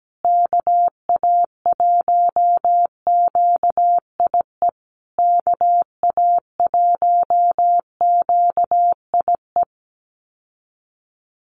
Morse code, heard as a keyed tone, about 17 wpm, 700 Hz. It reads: KA1QIE KA1QIE
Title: USA Call Signs Repeated 3x 17wpm